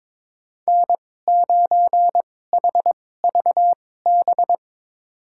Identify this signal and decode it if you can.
Morse code, heard as a keyed tone, about 22 wpm, spaced as for 16 wpm, 700 Hz. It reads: N9HVB
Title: USA Call Signs - 5 Characters - 2x Character Spacing - 22wpm